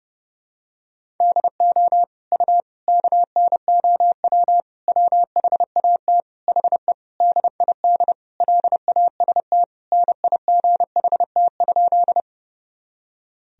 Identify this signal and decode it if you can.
Morse code, heard as a keyed tone, about 31 wpm, 700 Hz. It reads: DO U KNOW WHAT HE DID LAST NIGHT?
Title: Sentences from Top 300 Words and Abbreviations 30wpm